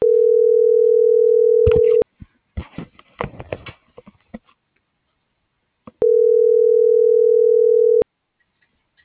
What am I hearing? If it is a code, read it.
Ringback tone (North America), 440 + 480 Hz.
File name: CENTRAL GOMSA / out aylin.bazan